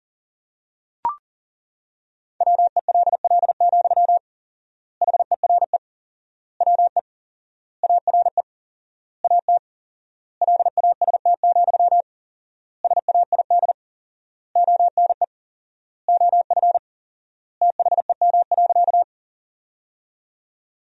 Morse code, heard as a keyed tone, about 40 wpm, 700 Hz. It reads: WELL, HERE WE ARE AT LAST, SAID ONE OF THEM.
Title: 4x Word-Spacing - Sentences from Top 300 Words - 40wpm